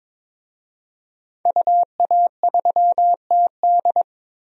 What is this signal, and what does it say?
Morse code, heard as a keyed tone, about 22 wpm, 700 Hz. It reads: UA3TD
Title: DXCC-100 Call Signs - 5 Characters - Repeated 3x - 22wpm